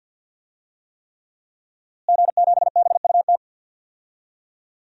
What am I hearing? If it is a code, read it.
Morse code, heard as a keyed tone, about 51 wpm, 700 Hz. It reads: G7BUT
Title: DXCC-100 Call Signs - 5 Characters - Repeated 3x - 50wpm